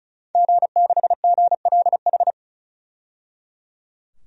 Morse code, heard as a keyed tone, about 36 wpm, 700 Hz. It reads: G6GLH
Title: DXCC-100 Call Signs - 5 Characters - 35wpm